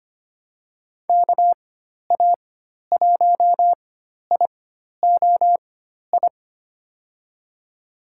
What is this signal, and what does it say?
Morse code, heard as a keyed tone, about 25 wpm, spaced as for 12 wpm, 700 Hz. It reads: KA1IOI
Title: USA Call Signs - 6 Characters - 4x Character Spacing - 25wpm